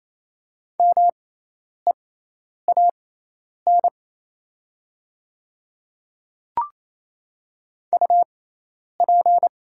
Morse code, heard as a keyed tone, about 28 wpm, spaced as for 10 wpm, 690 Hz. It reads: MEAN UP
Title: 6x Character Spacing - Top 200 Words - 28wpm